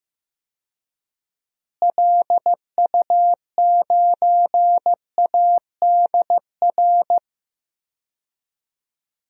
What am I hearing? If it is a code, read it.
Morse code, heard as a keyed tone, about 15 wpm, 700 Hz. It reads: LU9ADR
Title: DXCC-100 Call Signs - 6 Characters - Repeated 2x - 15wpm